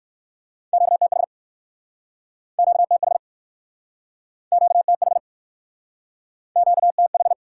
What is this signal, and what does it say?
Morse code, heard as a keyed tone, about 59 wpm, 700 Hz. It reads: QTH QTH QTH QTH